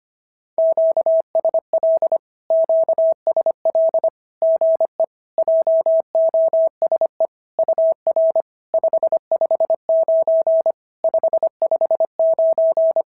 Morse code, heard as a keyed tone, about 25 wpm, 650 Hz. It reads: QSL QSL GE JOSE UR 559 559
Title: SOTA QSO Practice 25wpm